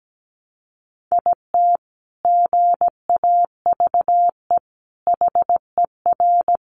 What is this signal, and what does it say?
Morse code, heard as a keyed tone, about 17 wpm, 700 Hz. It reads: IT GAVE HER
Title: Sentences from Top 1300 Words 17wpm